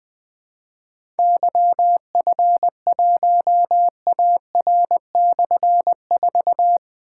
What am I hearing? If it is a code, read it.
Morse code, heard as a keyed tone, about 20 wpm, 700 Hz. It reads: YF1AR/4
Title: Difficult Call Signs 20wpm